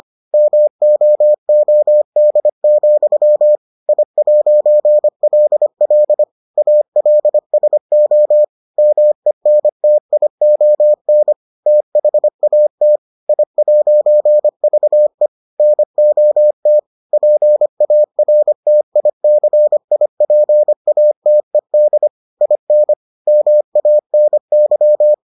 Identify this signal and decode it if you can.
Morse code, heard as a keyed tone, about 25 wpm, 600 Hz. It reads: MOOD, I'LL ALSO MENTION THAT I'VE NOT PARTICIPATED IN MANY